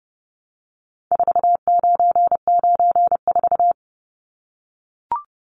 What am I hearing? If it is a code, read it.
Morse code, heard as a keyed tone, about 30 wpm, 700 Hz. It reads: 4994
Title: Numbers - 4-Digit 30wpm